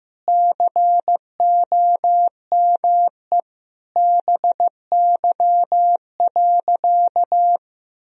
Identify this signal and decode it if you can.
Morse code, heard as a keyed tone, about 15 wpm, 700 Hz. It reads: COME BY.
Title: Sentences from Top 1000 Words - Review Part 3 - 15wpm